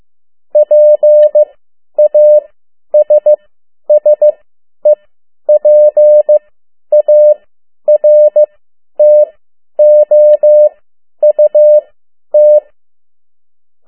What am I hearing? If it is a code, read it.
Morse code, heard as a keyed tone, about 15 wpm, spaced as for 10 wpm, 600 Hz. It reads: PASSEPARTOUT
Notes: Around the World in 80 Days paragraph one Morse code 10 WPM